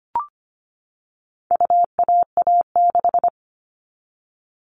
Morse code, heard as a keyed tone, about 25 wpm, 700 Hz. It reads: UAA6
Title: All Cut Numbers - 4-Digit 25wpm